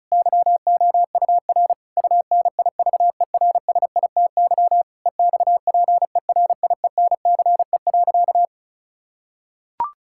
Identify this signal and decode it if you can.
Morse code, heard as a keyed tone, about 36 wpm, 700 Hz. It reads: YOUR UNIVERSITY EXPERIENCE.
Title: Sentences from Top 1000 Words - Review Part 5 - 35wpm